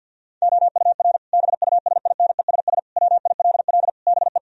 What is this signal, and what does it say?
Morse code, heard as a keyed tone, about 50 wpm, 700 Hz. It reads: OUR BUSINESS WILL BE
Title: Sentences from Top 1000 Words - Review Part 4 - 50wpm